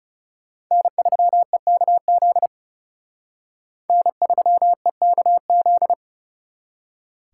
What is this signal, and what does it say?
Morse code, heard as a keyed tone, about 36 wpm, 700 Hz. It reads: N3EKZ N3EKZ
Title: Speed Racing - USA 1x3 Call Signs 30wpm